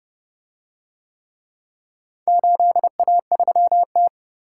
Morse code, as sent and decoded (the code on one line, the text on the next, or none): ---.. .- ...-- -
8A3T